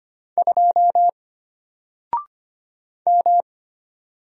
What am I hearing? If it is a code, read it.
Morse code, heard as a keyed tone, about 25 wpm, 700 Hz. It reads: 2 M
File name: Single-Letter-Number-25wpm